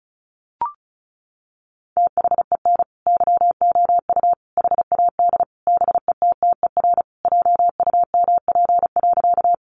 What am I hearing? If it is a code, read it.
Morse code, heard as a keyed tone, about 34 wpm, 700 Hz. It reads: THEN YOU HAD BETTER JUMP.
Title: Sentences from Top 1300 Words 35wpm